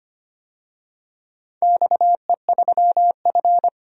Morse code, heard as a keyed tone, about 25 wpm, 700 Hz. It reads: XE3F
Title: DXCC-100 Call Signs - 4 Characters - 25wpm